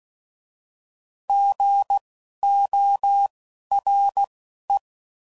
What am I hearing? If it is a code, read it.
Morse code, heard as a keyed tone, about 16 wpm, spaced as for 11 wpm, 800 Hz. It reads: GORE
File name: Prefixes / By First Letter